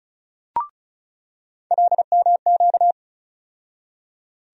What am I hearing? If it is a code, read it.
Morse code, heard as a keyed tone, about 34 wpm, 700 Hz. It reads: LMQ
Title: USA Call Sign Postfixes - 3 Letters 35wpm